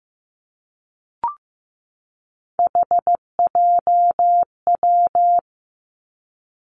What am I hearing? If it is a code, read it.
Morse code, heard as a keyed tone, about 15 wpm, 700 Hz. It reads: HJW
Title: Three Challenging Letter-Number Combinations 15wpm